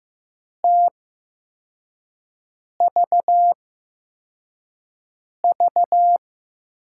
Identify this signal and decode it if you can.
Morse code, heard as a keyed tone, about 15 wpm, spaced as for 4 wpm, 700 Hz. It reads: TVV
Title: USA Call Signs - 6 Characters - 8x Character Spacing - 15wpm